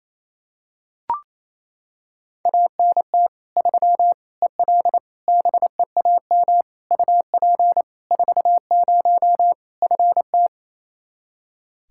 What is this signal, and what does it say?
Morse code, heard as a keyed tone, about 28 wpm, 700 Hz. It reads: ANT 3 EL BEAM UP 40 FT
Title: K1USN SST QSO Practice 28wpm